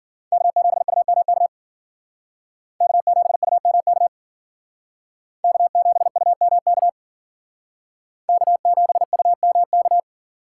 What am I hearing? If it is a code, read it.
Morse code, heard as a keyed tone, about 49 wpm, 700 Hz. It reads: K7UMK K7UMK K7UMK K7UMK
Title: Speed Racing - USA 1x3 Call Signs 40wpm